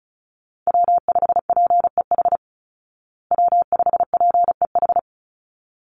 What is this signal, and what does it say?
Morse code, heard as a keyed tone, about 34 wpm, 700 Hz. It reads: W5PEH W5PEH
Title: Call Signs Repeated 3x - 35wpm